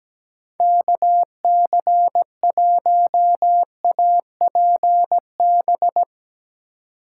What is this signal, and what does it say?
Morse code, heard as a keyed tone, about 17 wpm, 700 Hz. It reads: KC1APB
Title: USA Call Signs - 6 Characters - 4x Character Spacing - 17wpm